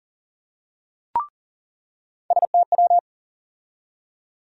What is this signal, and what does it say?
Morse code, heard as a keyed tone, about 40 wpm, 700 Hz. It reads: STW
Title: USA Call Sign Postfixes - 3 Letters 40wpm